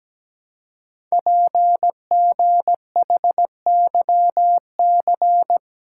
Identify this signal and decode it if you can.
Morse code, heard as a keyed tone, about 17 wpm, 700 Hz. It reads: PGHYC
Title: Code Groups - 5 Letters - 17wpm